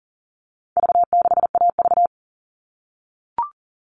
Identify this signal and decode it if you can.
Morse code, heard as a keyed tone, about 40 wpm, 700 Hz. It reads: V6AV
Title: All Cut Numbers - 4-Digit 40wpm